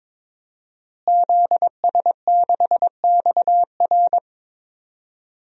Morse code, heard as a keyed tone, about 22 wpm, 700 Hz. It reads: ZS6XR
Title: DXCC-100 Call Signs - 5 Characters - Repeated 3x - 22wpm